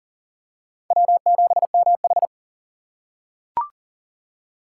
Morse code, heard as a keyed tone, about 40 wpm, 700 Hz. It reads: W7MH
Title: USA Call Signs - 6x Character Spacing - 40wpm